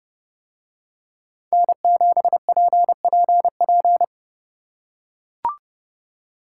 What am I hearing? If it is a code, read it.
Morse code, heard as a keyed tone, about 29 wpm, 700 Hz. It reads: N7PPP